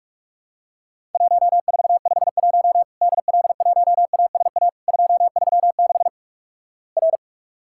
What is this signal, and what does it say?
Morse code, heard as a keyed tone, about 46 wpm, 700 Hz. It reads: 1451 DL1ASA 236 R
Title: Worked All Europe DX Contest - QTC Exchange - 45wpm